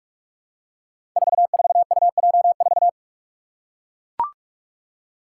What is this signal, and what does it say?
Morse code, heard as a keyed tone, about 44 wpm, 700 Hz. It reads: 44UJ4
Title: Five Challenging Letter-Number Combinations 45wpm